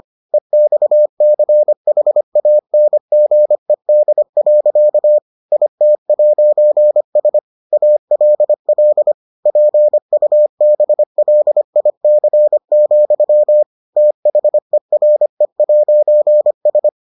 Morse code, heard as a keyed tone, about 25 wpm, 600 Hz. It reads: EXCHANGED. IT'S ALL PUBLIC, THERE'S